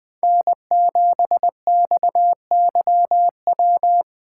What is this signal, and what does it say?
Morse code, heard as a keyed tone, about 20 wpm, 700 Hz. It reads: N7XYW